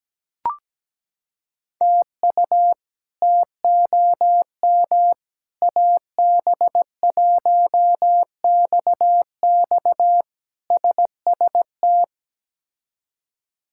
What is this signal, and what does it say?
Morse code, heard as a keyed tone, about 17 wpm, 700 Hz. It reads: TU TOM AB1XX SST
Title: K1USN SST QSO Practice - 17wpm